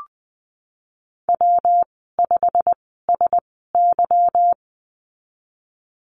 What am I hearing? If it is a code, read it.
Morse code, heard as a keyed tone, about 20 wpm, spaced as for 15 wpm, 700 Hz. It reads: W5SY
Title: 154 Lesson 25g - Letter p - Call Signs 20wpm